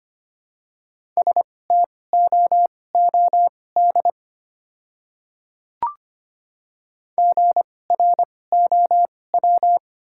Morse code, heard as a keyed tone, about 25 wpm, spaced as for 18 wpm, 700 Hz. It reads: STOOD GROW
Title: 2x Character Spacing - Top 500 Words - 25wpm